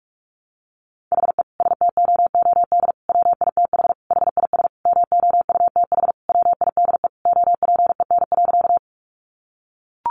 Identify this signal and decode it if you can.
Morse code, heard as a keyed tone, about 44 wpm, 700 Hz. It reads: HE STOOD WITH HIS MOUTH WIDE OPEN.
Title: Sentences from Top 1500 Words 45wpm